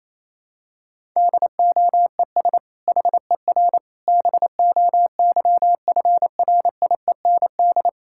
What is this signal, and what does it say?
Morse code, heard as a keyed tone, about 28 wpm, 700 Hz. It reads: DOES HER BOYFRIEND
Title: Sentences from Top 1000 Words - Review Part 5 - 28wpm